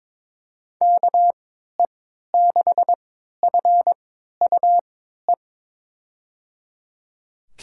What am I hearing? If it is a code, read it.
Morse code, heard as a keyed tone, about 22 wpm, spaced as for 13 wpm, 700 Hz. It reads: KE6FUE
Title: USA Call Signs - 6 Characters - 3x Character Spacing - 22wpm